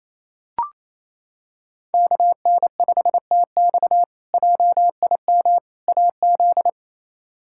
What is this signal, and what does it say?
Morse code, heard as a keyed tone, about 28 wpm, 700 Hz. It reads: KN5TX JIM AZ
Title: K1USN SST QSO Practice - 28wpm